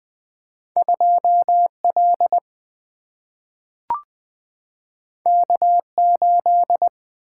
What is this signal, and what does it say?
Morse code, heard as a keyed tone, about 20 wpm, 700 Hz. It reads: 2 L K 8
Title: Single-Letter Single-Number 20wpm